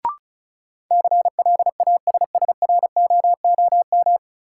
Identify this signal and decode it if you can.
Morse code, heard as a keyed tone, about 36 wpm, 700 Hz. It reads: CLASSROOM
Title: Sentences from Top 1500 Words - New Word Drill Alt - 35wpm